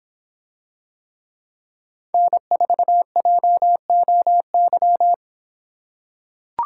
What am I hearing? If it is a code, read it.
Morse code, heard as a keyed tone, about 25 wpm, 700 Hz. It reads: N4JOY